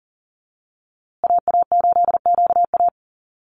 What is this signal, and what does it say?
Morse code, heard as a keyed tone, about 40 wpm, 700 Hz. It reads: AA8QA
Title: USA Call Signs Repeated 3x 40wpm